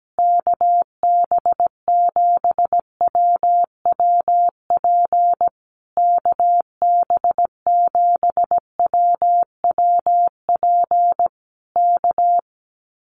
Morse code, heard as a keyed tone, about 17 wpm, 700 Hz. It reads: KB7WWP KB7WWP K
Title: Common QSO Phrases Encore 17wpm